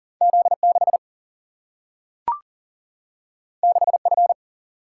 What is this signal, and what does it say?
Morse code, heard as a keyed tone, about 40 wpm, 700 Hz. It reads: Z 6 6 F
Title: Single-Letter Single-Number 40wpm